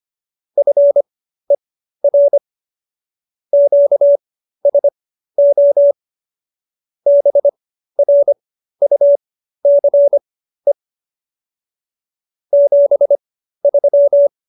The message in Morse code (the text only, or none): FER QSO BRUCE 73